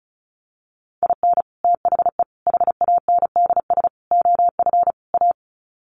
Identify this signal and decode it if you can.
Morse code, heard as a keyed tone, about 34 wpm, 700 Hz. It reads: IN THE HANDS OF A
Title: Sets of 5 Words Encore 35wpm